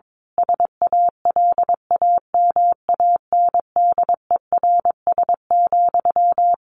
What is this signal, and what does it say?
Morse code, heard as a keyed tone, about 22 wpm, 700 Hz. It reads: SALAMANDERS,
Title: A Sound Of Thunder 22wpm